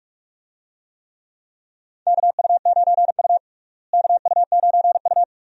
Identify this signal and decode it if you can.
Morse code, heard as a keyed tone, about 41 wpm, 700 Hz. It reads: KU9U KU9U
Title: USA Call Signs - 4 Characters - Repeated 3x - 45wpm